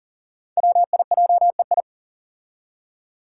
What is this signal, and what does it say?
Morse code, heard as a keyed tone, about 40 wpm, 700 Hz. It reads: WIJEI
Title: Five Challenging Letter-Number Combinations 40wpm